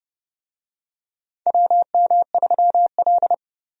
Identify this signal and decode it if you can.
Morse code, heard as a keyed tone, about 31 wpm, 700 Hz. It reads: WM3L
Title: USA Call Signs - 4 Characters - 30wpm